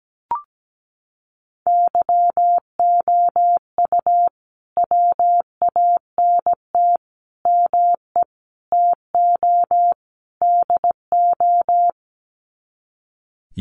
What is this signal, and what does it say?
Morse code, heard as a keyed tone, about 17 wpm, 700 Hz. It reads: YOU WANT ME TO DO
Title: Sets of 5 Words Encore2 17wpm